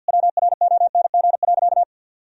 Morse code, heard as a keyed tone, about 50 wpm, 700 Hz. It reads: WRONG.
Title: Sentences from Top 1000 Words - Review Part 3 - 50wpm